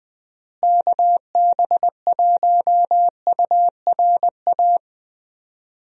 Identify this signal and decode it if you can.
Morse code, heard as a keyed tone, about 20 wpm, 700 Hz. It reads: KB1URA
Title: USA Call Signs - 6 Characters - Repeated 3x - 20wpm